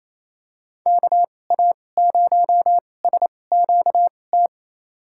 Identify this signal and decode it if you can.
Morse code, heard as a keyed tone, about 28 wpm, spaced as for 20 wpm, 700 Hz. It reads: KA0SQT
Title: USA Call Signs - 2x Character Spacing - 28wpm